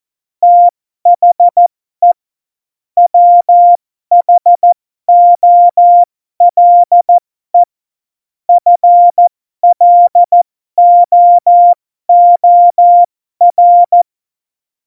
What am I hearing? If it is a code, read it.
Morse code, heard as a keyed tone, about 14 wpm, spaced as for 12 wpm, 700 Hz. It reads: THE WHOLE FLOOR
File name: Morsekurs / Witze